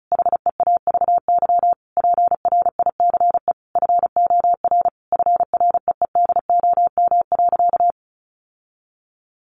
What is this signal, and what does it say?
Morse code, heard as a keyed tone, about 34 wpm, 700 Hz. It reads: HEAVY PRICE FOR FREEDOM.
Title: Sentences from Top 1100 Words 35wpm